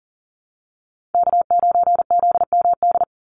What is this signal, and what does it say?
Morse code, heard as a keyed tone, about 40 wpm, 700 Hz. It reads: K9ZMD
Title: NA SOTA Chaser Callsigns 40wpm